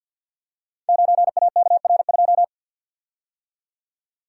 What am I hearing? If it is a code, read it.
Morse code, heard as a keyed tone, about 50 wpm, 700 Hz. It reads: 9AKR2
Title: Code Groups - 5 Letter:Numbers - 50wpm